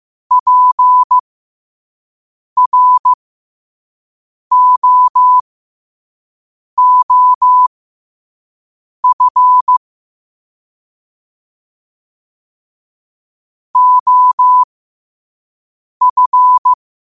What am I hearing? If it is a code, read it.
Morse code, heard as a keyed tone, about 15 wpm, spaced as for 5 wpm, 1000 Hz. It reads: PROOF OF
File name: qotdfiles / 5wpm